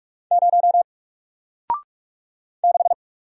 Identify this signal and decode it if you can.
Morse code, heard as a keyed tone, about 44 wpm, 700 Hz. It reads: 0 6
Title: Single Character (Letter:Number:Punctuation:Prosign) Alt 45wpm